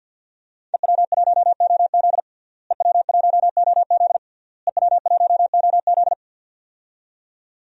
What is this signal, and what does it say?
Morse code, heard as a keyed tone, about 47 wpm, 700 Hz. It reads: EW1OZ EW1OZ EW1OZ
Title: DXCC-100 Call Signs - 5 Characters - Repeated 3x - 50wpm